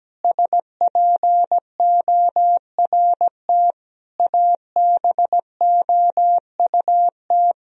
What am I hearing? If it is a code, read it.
Morse code, heard as a keyed tone, about 17 wpm, 700 Hz. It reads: SPORT ABOUT
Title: Sentences from Top 1000 Words - Review Part 4 - 17wpm